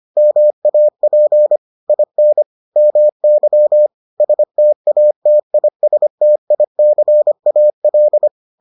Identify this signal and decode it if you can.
Morse code, heard as a keyed tone, about 25 wpm, 600 Hz. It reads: MAP IN MY STATISTICAL